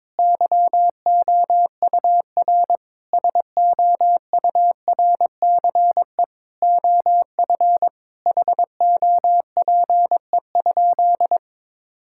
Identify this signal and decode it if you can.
Morse code, heard as a keyed tone, about 22 wpm, 700 Hz. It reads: YOUR SOURCE OF HOPE?
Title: Sentences from Top 1000 Words - Review Part 5 - 22wpm